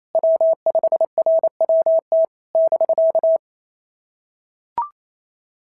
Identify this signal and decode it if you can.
Morse code, heard as a keyed tone, about 28 wpm, 660 Hz. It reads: W5RWT <BK>
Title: SOTA QSO Practice 28wpm